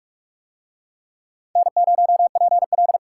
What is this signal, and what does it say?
Morse code, heard as a keyed tone, about 42 wpm, 700 Hz. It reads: N0PL